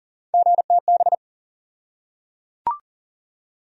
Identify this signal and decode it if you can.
Morse code, heard as a keyed tone, about 37 wpm, 700 Hz. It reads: GTB
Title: USA Call Sign Postfixes - 3 Letters 40wpm